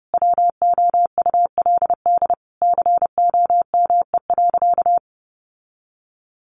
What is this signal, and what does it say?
Morse code, heard as a keyed tone, about 30 wpm, 700 Hz. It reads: WOULD COME.
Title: Sentences from Top 1500 Words 30wpm